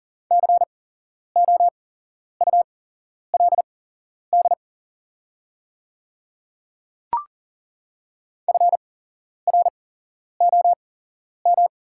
Morse code, heard as a keyed tone, about 40 wpm, spaced as for 11 wpm, 700 Hz. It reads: COULD FROM
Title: 8x Character Spacing - Top 100 Words - 40wpm